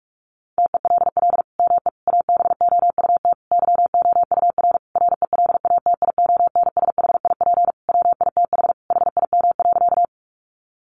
Morse code, heard as a keyed tone, about 44 wpm, 700 Hz. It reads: TELL ME ABOUT YOUR RELATIONSHIP WITH HIM.
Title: Sentences from Top 1100 Words 45wpm